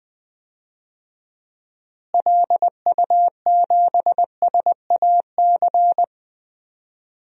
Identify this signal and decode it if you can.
Morse code, heard as a keyed tone, about 21 wpm, 700 Hz. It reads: LU7SAC
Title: DXCC-100 Call Signs - 6 Characters - Repeated 3x - 20wpm